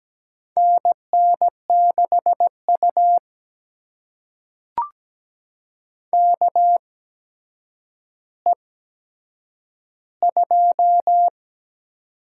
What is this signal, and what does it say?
Morse code, heard as a keyed tone, about 17 wpm, spaced as for 8 wpm, 700 Hz. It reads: NN6U KE2
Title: USA Call Signs - 4 Characters - 8x Character Spacing - 17wpm